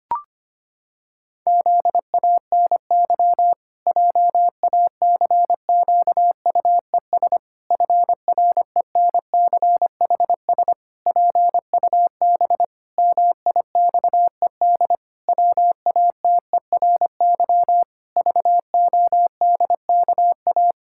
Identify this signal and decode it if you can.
Morse code, heard as a keyed tone, about 25 wpm, 700 Hz. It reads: ZANY JACQUES FRENCHS PUB MIXED WATERY VODKA
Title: Pangram Challenge v2 - 25wpm